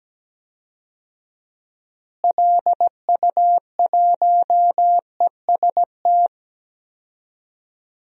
Morse code, heard as a keyed tone, about 17 wpm, 700 Hz. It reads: LU1EST
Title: DXCC-100 Call Signs - 6 Characters - Repeated 3x - 17wpm